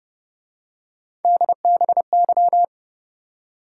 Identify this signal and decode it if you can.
Morse code, heard as a keyed tone, about 28 wpm, 700 Hz. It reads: DBY